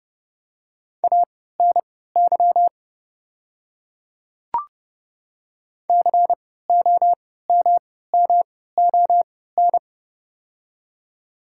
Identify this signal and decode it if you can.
Morse code, heard as a keyed tone, about 31 wpm, spaced as for 17 wpm, 700 Hz. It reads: ANY COMMON
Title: 3x Character Spacing - Top 500 Words - 30wpm